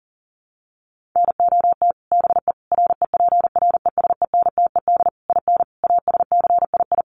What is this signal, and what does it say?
Morse code, heard as a keyed tone, about 40 wpm, 700 Hz. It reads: NOT BE REPRESENTED IN ASCII
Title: Sets of 5 Words 40wpm